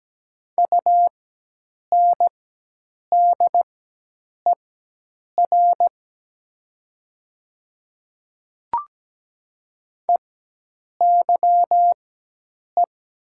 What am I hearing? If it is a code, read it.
Morse code, heard as a keyed tone, about 17 wpm, spaced as for 8 wpm, 700 Hz. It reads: UNDER EYE
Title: 4x Character Spacing - Top 300 Words - 17wpm